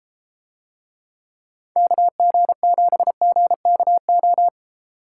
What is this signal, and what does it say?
Morse code, heard as a keyed tone, about 31 wpm, 700 Hz. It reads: KG7GKO